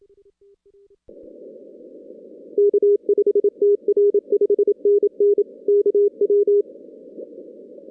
Morse code, heard as a keyed tone, about 27 wpm, 410 Hz. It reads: K5TR 5NN KW
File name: 2016 arrl dx cw / 2016-ARRL-DX-CW-K5TR- - 0-46